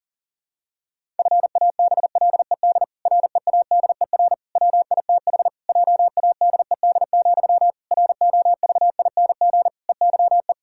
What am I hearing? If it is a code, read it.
Morse code, heard as a keyed tone, about 40 wpm, 700 Hz. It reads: FABLED READER WITH JADED, ROVING EYE
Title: Pangram Challenge v2 - 40wpm